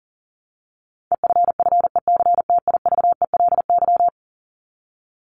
Morse code, heard as a keyed tone, about 40 wpm, 700 Hz. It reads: EFFECTIVELY
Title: Top 1251-1500 Words 40wpm